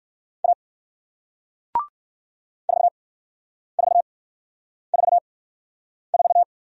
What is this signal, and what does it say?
Morse code, heard as a keyed tone, about 56 wpm, spaced as for 9 wpm, 700 Hz. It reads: A 4444
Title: Speed-Racing Single Letter-Number 45wpm